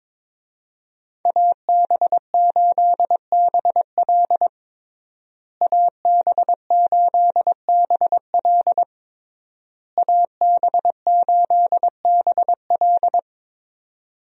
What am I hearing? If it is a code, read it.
Morse code, heard as a keyed tone, about 21 wpm, 700 Hz. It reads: AB8BL AB8BL AB8BL
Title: USA Call Signs - 5 Characters - Repeated 3x - 22wpm